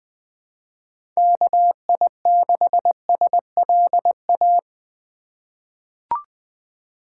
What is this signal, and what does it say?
Morse code, heard as a keyed tone, about 20 wpm, 700 Hz. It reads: KI6SLA
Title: 2023 SOTA Chaser Call Signs - 20wpm